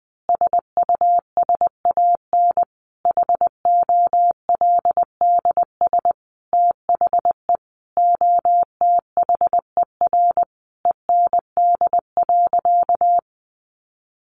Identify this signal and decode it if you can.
Morse code, heard as a keyed tone, about 20 wpm, 700 Hz. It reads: SUSAN HOLDS THE OTHER END.
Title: Easy Phrases 20wpm